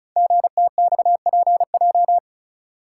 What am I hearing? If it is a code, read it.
Morse code, heard as a keyed tone, about 36 wpm, 700 Hz. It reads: GTXPJ
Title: Code Groups - 5 Letters - 35wpm